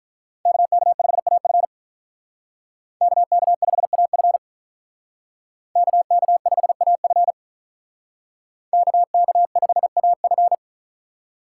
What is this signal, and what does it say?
Morse code, heard as a keyed tone, about 44 wpm, 700 Hz. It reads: KK5AF KK5AF KK5AF KK5AF